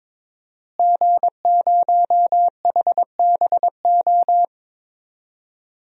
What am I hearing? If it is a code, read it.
Morse code, heard as a keyed tone, about 22 wpm, 700 Hz. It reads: G0HBO
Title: DXCC-100 Call Signs - 5 Characters - Repeated 3x - 22wpm